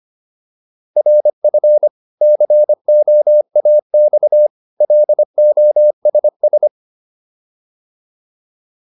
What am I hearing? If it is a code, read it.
Morse code, heard as a keyed tone, about 25 wpm, 600 Hz. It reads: RF COAX LOSS